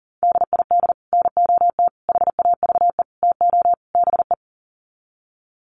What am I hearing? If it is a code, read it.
Morse code, heard as a keyed tone, about 40 wpm, 700 Hz. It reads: DID NOT HAVE TO BE
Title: Sets of 5 Words Encore2 40wpm